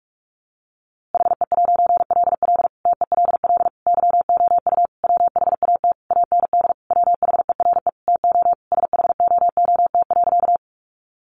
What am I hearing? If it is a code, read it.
Morse code, heard as a keyed tone, about 44 wpm, 700 Hz. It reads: HE'LL TELL YOU WHAT AND WHERE TO SHOOT.
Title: A Sound Of Thunder 45wpm